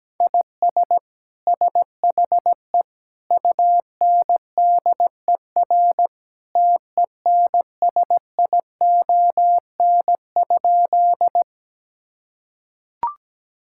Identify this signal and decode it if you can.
Morse code, heard as a keyed tone, about 17 wpm, 700 Hz. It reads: IS SHE UNDER TENSION?
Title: Sentences from Top 1600 Words 17wpm